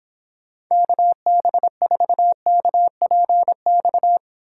Morse code, heard as a keyed tone, about 26 wpm, 700 Hz. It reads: KB4KPX